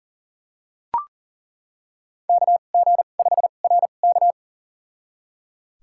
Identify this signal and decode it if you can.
Morse code, heard as a keyed tone, about 40 wpm, spaced as for 29 wpm, 700 Hz. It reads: KG5RK
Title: USA Call Signs - 5 Characters - 2x Character Spacing - 40wpm